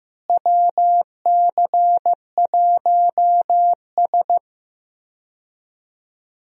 Morse code, heard as a keyed tone, about 15 wpm, 700 Hz. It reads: WC1S
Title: USA Call Signs - 4 Characters - Repeated 3x - 15wpm